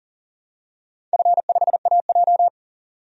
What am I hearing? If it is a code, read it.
Morse code, heard as a keyed tone, about 40 wpm, 700 Hz. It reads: F5AJ
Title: DXCC-100 Call Signs - 4 Characters - 40wpm